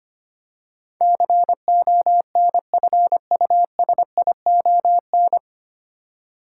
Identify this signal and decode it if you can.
Morse code, heard as a keyed tone, about 25 wpm, 700 Hz. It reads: CONFUSION
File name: practice / 25wpm / 9 Letter Words - First 3 Letters Given - 25wpm